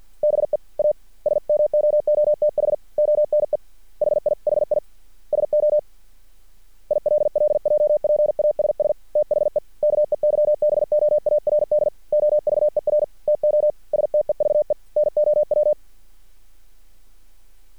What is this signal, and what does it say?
Morse code, heard as a keyed tone, about 50 wpm, 600 Hz. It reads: BE A SMOOTH ONE HIHI SO ILLJPASS THE KEYBOARD OVER TO STEVE NOW
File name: archived / CRSnet